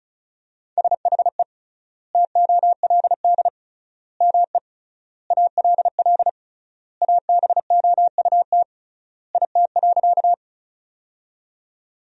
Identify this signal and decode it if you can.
Morse code, heard as a keyed tone, about 36 wpm, 700 Hz. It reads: SHE TOLD ME ALL ABOUT IT.
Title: 3x Word-Spacing - Sentences from Top 400 Words - 35wpm